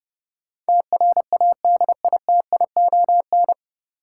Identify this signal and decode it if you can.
Morse code, heard as a keyed tone, about 30 wpm, 700 Hz. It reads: TRADITION
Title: Sentences from Top 1500 Words - New Word Drill Alt - 30wpm